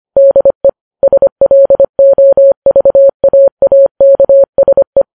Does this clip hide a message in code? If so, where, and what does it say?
Yes: Morse code from 0.2 to 5.0 s, heard as a keyed tone, about 25 wpm, 550 Hz. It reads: DE SLOVAAKSE